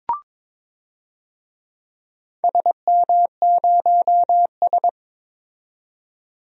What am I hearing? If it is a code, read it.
Morse code, heard as a keyed tone, about 21 wpm, 700 Hz. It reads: SM0S
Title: DXCC-100 Call Signs - 4 Characters - 22wpm